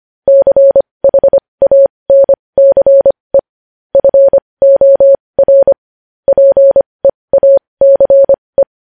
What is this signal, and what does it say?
Morse code, heard as a keyed tone, about 25 wpm, spaced as for 20 wpm, 550 Hz. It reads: CHANCE FOR PEACE
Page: news-headlines-in-morse-code-at-20-wpm